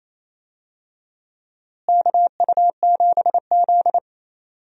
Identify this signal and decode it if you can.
Morse code, heard as a keyed tone, about 28 wpm, 700 Hz. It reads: KU7Z
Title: USA Call Signs - 4 Characters - Repeated 3x - 28wpm